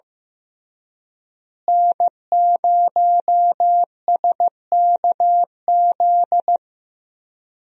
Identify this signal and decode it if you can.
Morse code, heard as a keyed tone, about 15 wpm, 700 Hz. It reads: N0SKZ
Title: USA Call Signs - 5 Characters - Repeated 3x - 15wpm